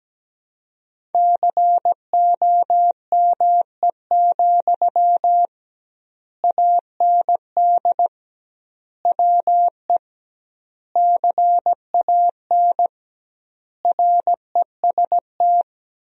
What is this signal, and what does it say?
Morse code, heard as a keyed tone, about 17 wpm, 700 Hz. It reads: COME, AND WE CAN REST
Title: 2x Word-Spacing - Sentences from Top 500 Words - 17wpm